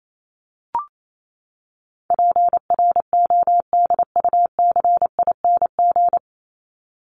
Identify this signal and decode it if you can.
Morse code, heard as a keyed tone, about 28 wpm, 700 Hz. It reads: PRODUCING
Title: ING Suffix - Top 100 Words with Base Words 28wpm